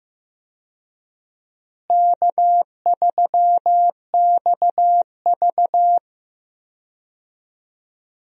Morse code, heard as a keyed tone, about 15 wpm, 700 Hz. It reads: K3XV
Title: USA Call Signs - 4 Characters - Repeated 2x - 15wpm